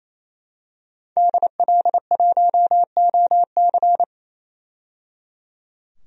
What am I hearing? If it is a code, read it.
Morse code, heard as a keyed tone, about 28 wpm, 700 Hz. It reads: DL1OC
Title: DXCC-100 Call Signs - 5 Characters - 28wpm